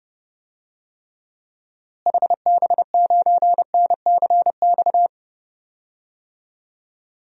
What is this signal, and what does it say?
Morse code, heard as a keyed tone, about 30 wpm, 700 Hz. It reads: HB9NCX
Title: DXCC-100 Call Signs - 6 Characters - Repeated 2x - 30wpm